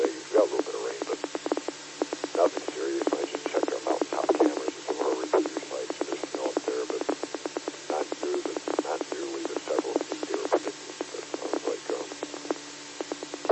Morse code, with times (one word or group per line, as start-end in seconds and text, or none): none